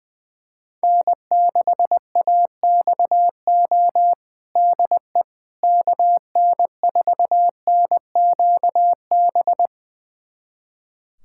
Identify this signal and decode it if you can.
Morse code, heard as a keyed tone, about 20 wpm, 700 Hz. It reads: N6AXO DE KN4NQB
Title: USA Callsign DE Callsign 20wpm